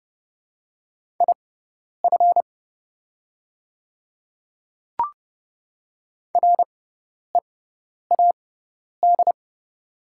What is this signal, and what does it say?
Morse code, heard as a keyed tone, about 30 wpm, spaced as for 10 wpm, 690 Hz. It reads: IF READ
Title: 6x Character Spacing - Top 500 Words - 30wpm